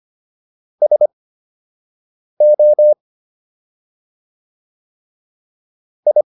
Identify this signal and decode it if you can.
Morse code, heard as a keyed tone, about 25 wpm, spaced as for 6 wpm, 600 Hz. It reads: SO I